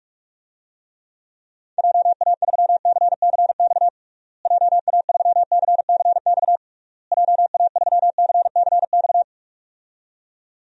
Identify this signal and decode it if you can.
Morse code, heard as a keyed tone, about 47 wpm, 700 Hz. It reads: JA3CCX JA3CCX JA3CCX
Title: DXCC-100 Call Signs - 6 Characters - Repeated 3x - 45wpm